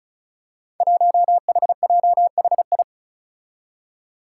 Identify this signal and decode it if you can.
Morse code, heard as a keyed tone, about 36 wpm, 700 Hz. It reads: 1HJHI
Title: Five Challenging Letter-Number Combinations 35wpm